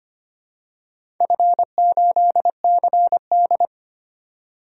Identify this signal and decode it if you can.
Morse code, heard as a keyed tone, about 25 wpm, 700 Hz. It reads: F8CD
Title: DXCC-100 Call Signs - 4 Characters - 25wpm